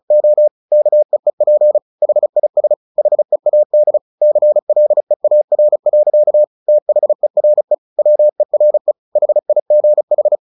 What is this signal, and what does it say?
Morse code, heard as a keyed tone, about 34 wpm, 600 Hz. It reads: O KEEP HIS HEAD CLEAR. THERE WERE HIGH
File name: the old man and the sea 2 35wpm